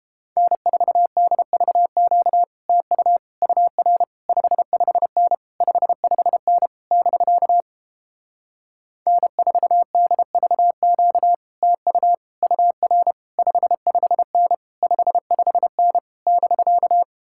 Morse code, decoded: N4DVQ TU UR 55N 55N <BK> N4DVQ TU UR 55N 55N <BK>

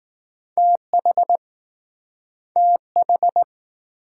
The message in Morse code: TH TH

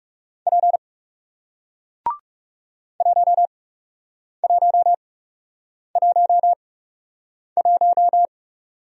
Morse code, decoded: P 1111

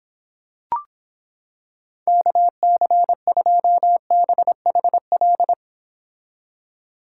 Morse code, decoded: KC2BHL

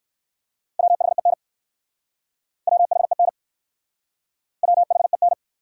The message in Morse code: WHEN WHEN WHEN

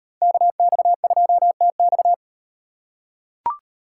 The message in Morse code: KX2TX

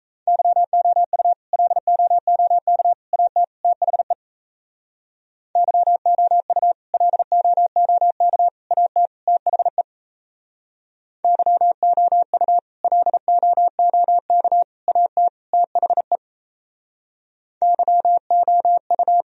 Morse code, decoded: YOU LOOK AT THE YOU LOOK AT THE YOU LOOK AT THE YOU

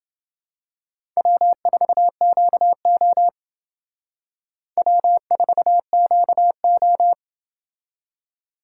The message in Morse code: W4QO W4QO

